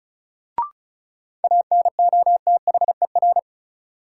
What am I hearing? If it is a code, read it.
Morse code, heard as a keyed tone, about 36 wpm, 700 Hz. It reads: ANOTHER
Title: Sets of 5 Words Encore - New Word Drill - 35wpm